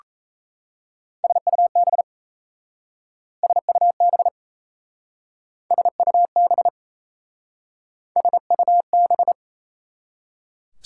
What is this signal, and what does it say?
Morse code, heard as a keyed tone, about 35 wpm, 700 Hz. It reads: SUB SUB SUB SUB